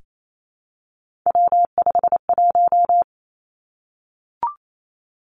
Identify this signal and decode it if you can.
Morse code, heard as a keyed tone, about 28 wpm, 700 Hz. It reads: W51
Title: Three Challenging Letter-Number Combinations 28wpm